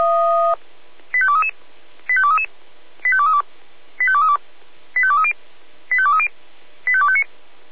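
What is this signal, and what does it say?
Five-tone selective call (ZVEI), heard as quick runs of five tones, tones of about 70 ms. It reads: 85233 85233 85232 85232 85230 85230 85259
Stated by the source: ZVEI five-tone call received by SDR, archived by Feuerwehr Orlamünde.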